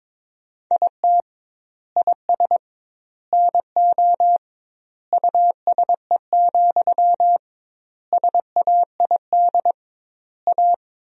Morse code, heard as a keyed tone, about 22 wpm, 700 Hz. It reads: IT IS NO USE, SAID A
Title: 2x Word-Spacing - Sentences from Top 500 Words - 22wpm